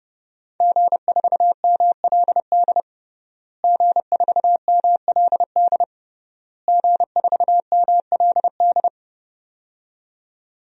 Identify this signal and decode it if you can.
Morse code, heard as a keyed tone, about 31 wpm, 700 Hz. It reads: G4MLD G4MLD G4MLD
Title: DXCC-100 Call Signs - 5 Characters - Repeated 3x - 30wpm